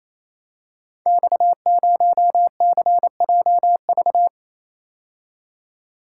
Morse code, heard as a keyed tone, about 29 wpm, 700 Hz. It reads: X0CJV
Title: Code Groups - 5 Letter:Numbers - 28wpm